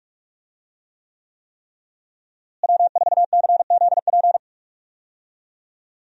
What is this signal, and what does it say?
Morse code, heard as a keyed tone, about 45 wpm, 700 Hz. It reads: W4CZP
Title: USA Call Signs - 5 Characters - Repeated 3x - 45wpm